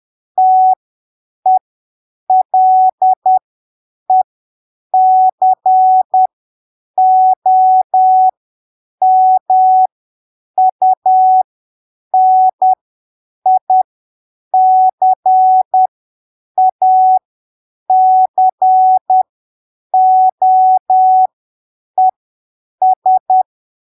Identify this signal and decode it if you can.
Morse code, heard as a keyed tone, about 10 wpm, spaced as for 7 wpm, 750 Hz. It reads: TELECOMUNICACOES
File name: arquivos / texto / texto 22 10ppm